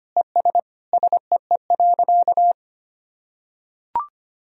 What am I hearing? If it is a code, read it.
Morse code, heard as a keyed tone, about 25 wpm, 700 Hz. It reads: ES SEE.
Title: Sentences from Top 100 Words and Abbreviations 25wpm